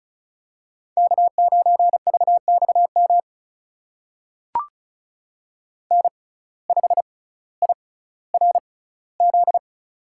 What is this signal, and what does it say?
Morse code, heard as a keyed tone, about 34 wpm, spaced as for 18 wpm, 700 Hz. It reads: K9VXM N5IRZ
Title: USA Call Signs - 5 Characters - 6x Character Spacing - 35wpm